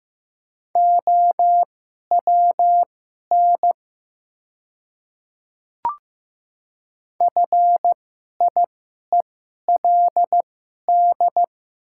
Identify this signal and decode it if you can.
Morse code, heard as a keyed tone, about 15 wpm, spaced as for 11 wpm, 700 Hz. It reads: OWN FIELD
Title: 2x Character Spacing - Top 500 Words - 15wpm